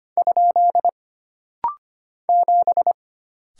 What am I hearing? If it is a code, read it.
Morse code, heard as a keyed tone, about 25 wpm, 700 Hz. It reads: ? 7
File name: Single Character (Letter:Number:Punctuation:Prosign) - Rapid-Fire - 25wpm